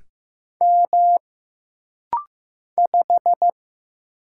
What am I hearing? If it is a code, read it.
Morse code, heard as a keyed tone, about 15 wpm, 700 Hz. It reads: M 5
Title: Single Letter-Number Alt - Rapid-Fire 15wpm